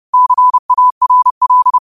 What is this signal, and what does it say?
Morse code, heard as a keyed tone, about 30 wpm, 1000 Hz. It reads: CARL